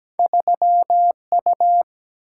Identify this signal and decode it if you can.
Morse code, heard as a keyed tone, about 17 wpm, 700 Hz. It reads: 3U
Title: Single-Letter Single-Number 17wpm